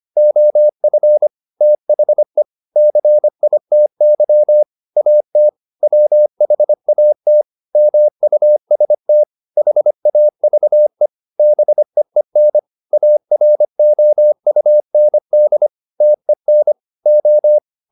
Morse code, heard as a keyed tone, about 25 wpm, 600 Hz. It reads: OF THE CITY AT WHAT MUST HAVE BEEN AROUND TEN O